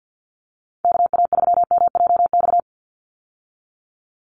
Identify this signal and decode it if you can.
Morse code, heard as a keyed tone, about 50 wpm, 700 Hz. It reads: KA3MJX
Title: USA Call Signs 50wpm